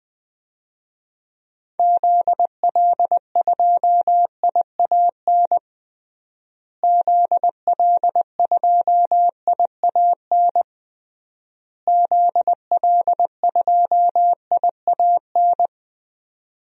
Morse code, heard as a keyed tone, about 21 wpm, 700 Hz. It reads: ZL2IAN ZL2IAN ZL2IAN
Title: DXCC-100 Call Signs - 6 Characters - Repeated 3x - 20wpm